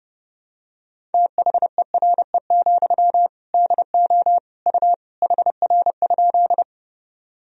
Morse code, heard as a keyed tone, about 30 wpm, 700 Hz. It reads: THERE, DO U HR?
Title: Sentences from Top 300 Words and Abbreviations 30wpm